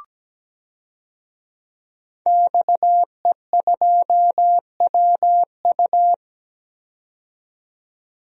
Morse code, heard as a keyed tone, about 17 wpm, 700 Hz. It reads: XE2WU
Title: DXCC-100 Call Signs - 5 Characters - 17wpm